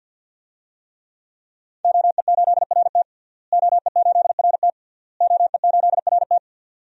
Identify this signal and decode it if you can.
Morse code, heard as a keyed tone, about 52 wpm, 700 Hz. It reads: OE8RT OE8RT OE8RT
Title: DXCC-100 Call Signs - 5 Characters - Repeated 3x - 50wpm